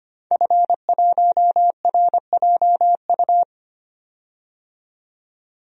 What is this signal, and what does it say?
Morse code, heard as a keyed tone, about 25 wpm, 700 Hz. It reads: F1RJU
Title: DXCC-100 Call Signs - 5 Characters - 25wpm